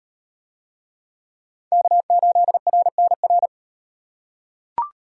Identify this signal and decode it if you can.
Morse code, heard as a keyed tone, about 35 wpm, 700 Hz. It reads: K8RNR